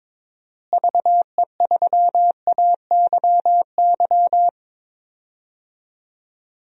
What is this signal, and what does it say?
Morse code, heard as a keyed tone, about 21 wpm, 700 Hz. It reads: VE3AYY
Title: DXCC-100 Call Signs - 6 Characters - Repeated 3x - 22wpm